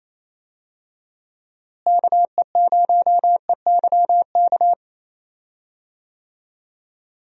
Morse code, heard as a keyed tone, about 28 wpm, 700 Hz. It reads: KE0EYK
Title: USA Call Signs - 6 Characters - Repeated 3x - 28wpm